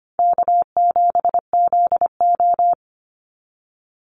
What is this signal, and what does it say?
Morse code, heard as a keyed tone, about 25 wpm, 700 Hz. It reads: K7ZO
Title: NA SOTA Chaser Callsigns 25wpm